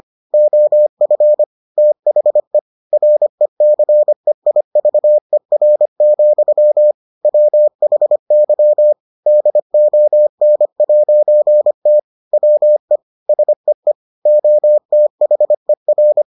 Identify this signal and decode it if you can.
Morse code, heard as a keyed tone, about 25 wpm, 600 Hz. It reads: OF THE RECEIVER, WHY DON'T WE SEE OTHER